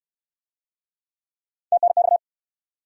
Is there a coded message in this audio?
Yes: Morse code from 1.7 to 2.2 s, heard as a keyed tone, about 63 wpm, 700 Hz. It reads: TNX